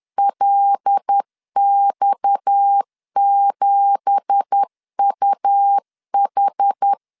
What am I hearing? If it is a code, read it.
Morse code, heard as a keyed tone, about 10 wpm, 800 Hz. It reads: LX7UH
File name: LEZIONI / 9 2867 10wpm 2733